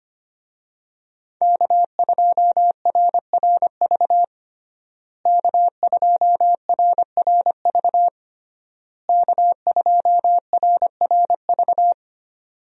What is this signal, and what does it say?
Morse code, heard as a keyed tone, about 25 wpm, 700 Hz. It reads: K2RRV K2RRV K2RRV
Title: USA Call Signs - 5 Characters - Repeated 3x - 25wpm